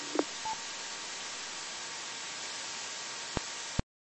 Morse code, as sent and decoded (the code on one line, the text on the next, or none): none